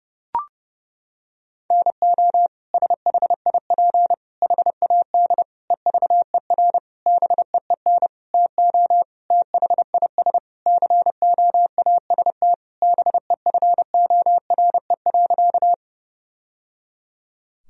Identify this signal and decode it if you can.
Morse code, heard as a keyed tone, about 30 wpm, 700 Hz. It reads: NO SHIP HAD EVER BEEN TO THIS COAST BEFORE.
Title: Sentences from Top 1000 Words - Review Part 4 - 30wpm